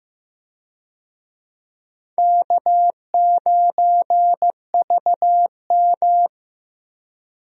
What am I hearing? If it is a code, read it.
Morse code, heard as a keyed tone, about 15 wpm, 700 Hz. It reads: K9VM
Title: USA Call Signs - 4 Characters - Repeated 3x - 15wpm